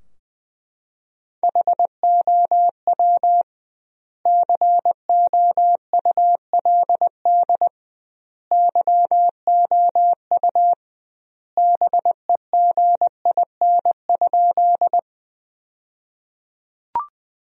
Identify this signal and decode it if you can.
Morse code, heard as a keyed tone, about 20 wpm, 700 Hz. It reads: HOW COULD YOU BEGIN?
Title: 2x Word-Spacing - Sentences from Top 300 Words - 20wpm